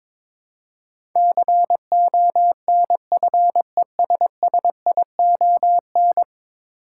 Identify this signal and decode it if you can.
Morse code, heard as a keyed tone, about 22 wpm, 700 Hz. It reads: CONFESSION
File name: practice / 22wpm / 10 Letter Words - First 3 Letters Given - 22wpm